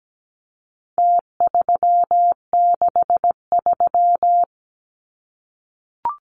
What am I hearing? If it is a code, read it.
Morse code, heard as a keyed tone, about 17 wpm, 700 Hz. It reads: T363
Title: Numbers - Contest Style - 4-Digit 17wpm